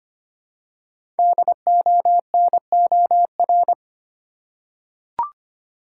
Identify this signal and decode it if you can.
Morse code, heard as a keyed tone, about 25 wpm, 700 Hz. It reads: DONOR
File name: 5 Letter Words - First 3 Letters Given - 25wpm